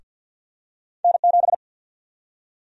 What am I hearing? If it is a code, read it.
Morse code, heard as a keyed tone, about 51 wpm, 700 Hz. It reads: N7
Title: USA Call Sign Prefixes - Repeated 2x - 50wpm